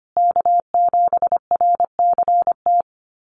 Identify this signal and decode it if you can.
Morse code, heard as a keyed tone, about 25 wpm, 700 Hz. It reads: K7RCT
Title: USA Call Signs Repeated 3x 25wpm